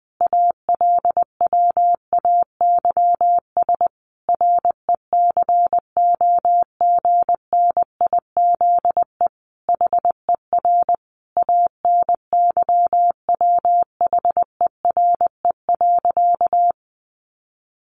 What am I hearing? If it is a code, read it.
Morse code, heard as a keyed tone, about 20 wpm, 700 Hz. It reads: ALWAYS RECOGNIZE HER ANYWHERE.
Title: Sentences from Top 1400 Words 20wpm